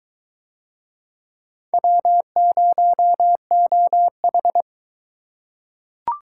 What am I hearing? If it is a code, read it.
Morse code, heard as a keyed tone, about 22 wpm, 700 Hz. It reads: W0OH